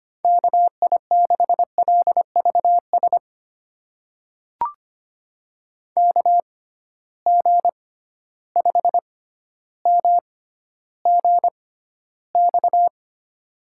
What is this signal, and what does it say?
Morse code, heard as a keyed tone, about 25 wpm, spaced as for 13 wpm, 700 Hz. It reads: KI6LVS KG5MGX
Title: USA Call Signs - 6 Characters - 6x Character Spacing - 25wpm